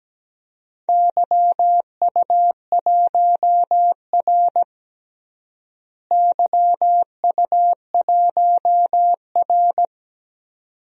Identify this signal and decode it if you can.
Morse code, heard as a keyed tone, about 17 wpm, 700 Hz. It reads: YU1R YU1R
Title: DXCC-100 Call Signs - 4 Characters - Repeated 2x - 17wpm